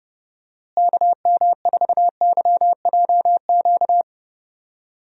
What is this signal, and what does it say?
Morse code, heard as a keyed tone, about 32 wpm, 700 Hz. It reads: KM4YJQ
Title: USA Call Signs - 6 Characters - Repeated 3x - 30wpm